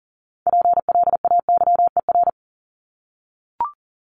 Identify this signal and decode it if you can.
Morse code, heard as a keyed tone, about 40 wpm, 700 Hz. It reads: PLAYER
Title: Top 501-750 Words 40wpm